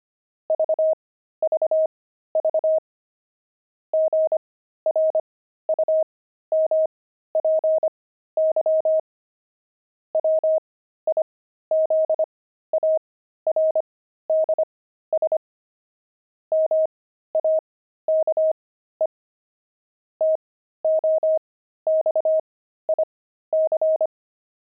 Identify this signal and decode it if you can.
Morse code, heard as a keyed tone, about 25 wpm, spaced as for 13 wpm, 630 Hz. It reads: VVV GRUMPY WIZARDS MAKE TOXIC
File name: pangram-fs-13-cs-25-grumpy